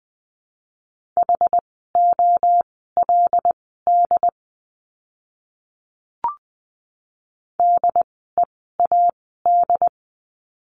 Morse code, heard as a keyed tone, about 20 wpm, spaced as for 14 wpm, 700 Hz. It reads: HOLD DEAD
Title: 075 Lesson 13f - Letter L - Words 20wpm